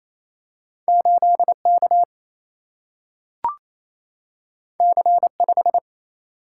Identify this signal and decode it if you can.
Morse code, heard as a keyed tone, about 28 wpm, 700 Hz. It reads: 8K C5
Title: Single-Letter Single-Number 28wpm